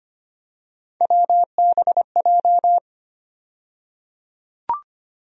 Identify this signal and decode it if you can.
Morse code, heard as a keyed tone, about 25 wpm, 700 Hz. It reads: WBJ
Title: USA Call Sign Postfixes - 3 Letters 25wpm